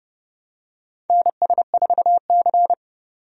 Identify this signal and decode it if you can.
Morse code, heard as a keyed tone, about 29 wpm, 700 Hz. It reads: NS4C
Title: USA Call Signs - 4 Characters - 30wpm